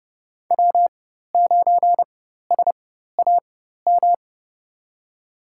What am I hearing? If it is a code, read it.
Morse code, heard as a keyed tone, about 32 wpm, spaced as for 14 wpm, 700 Hz. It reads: W9SAM
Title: USA Call Signs - 4x Character Spacing - 30wpm